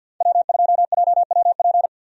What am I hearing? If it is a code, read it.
Morse code, heard as a keyed tone, about 46 wpm, 700 Hz. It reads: W2JWP
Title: USA Call Signs - 5 Characters - 50wpm